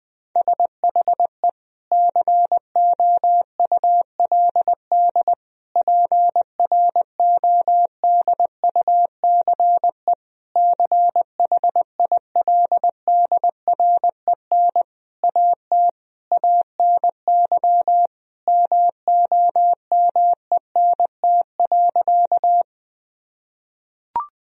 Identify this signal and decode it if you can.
Morse code, heard as a keyed tone, about 21 wpm, 700 Hz. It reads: SHE COULD PRODUCE CHILDREN AT ANY MOMENT.
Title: Sentences from Top 1000 Words - Review Part 3 - 20wpm